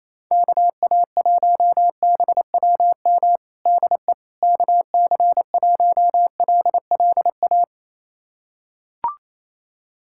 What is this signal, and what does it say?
Morse code, heard as a keyed tone, about 28 wpm, 700 Hz. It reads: KA1BWM DE KC1LLA
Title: USA Callsign DE Callsign 28wpm